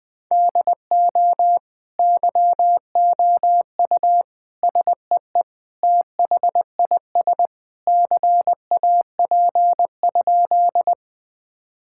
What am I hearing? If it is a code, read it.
Morse code, heard as a keyed tone, about 20 wpm, 700 Hz. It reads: DO YOU SEE THIS CAP?
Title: Sentences from Top 1600 Words 20wpm